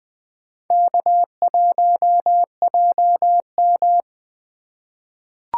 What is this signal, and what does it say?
Morse code, heard as a keyed tone, about 20 wpm, 700 Hz. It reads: K1JM
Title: USA Call Signs - 3x Character Spacing - 20wpm